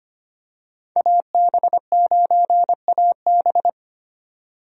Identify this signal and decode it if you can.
Morse code, heard as a keyed tone, about 25 wpm, 700 Hz. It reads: AB9AB
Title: USA Call Signs - 5 Characters - 25wpm